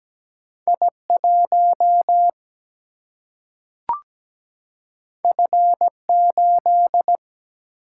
Morse code, heard as a keyed tone, about 17 wpm, 700 Hz. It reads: I1 F8
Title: Single-Letter Single-Number 17wpm